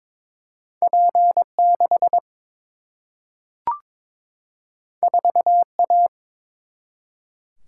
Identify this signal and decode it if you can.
Morse code, heard as a keyed tone, about 22 wpm, 700 Hz. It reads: P6 4A
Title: Single-Letter Single-Number 22wpm